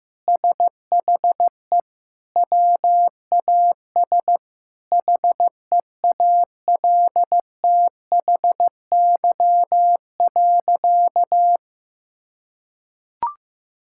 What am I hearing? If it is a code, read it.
Morse code, heard as a keyed tone, about 16 wpm, 700 Hz. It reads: SHE WAS HEALTHY.
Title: Sentences from Top 1000 Words - Review Part 5 - 15wpm